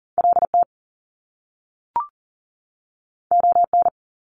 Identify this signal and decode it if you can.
Morse code, heard as a keyed tone, about 40 wpm, 700 Hz. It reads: LT ON
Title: Top Two Letter Combinations in Words 40wpm